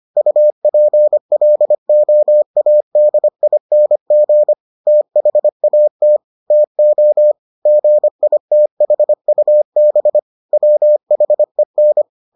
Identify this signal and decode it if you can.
Morse code, heard as a keyed tone, about 25 wpm, 600 Hz. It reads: UPLOADING THAT TO GITHUB WHEN